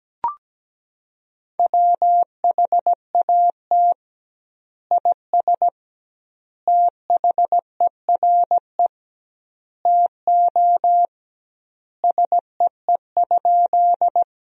Morse code, heard as a keyed tone, about 17 wpm, 700 Hz. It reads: WHAT IS THERE TO SEE?
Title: 2x Word-Spacing - Sentences from Top 100 Words - 17wpm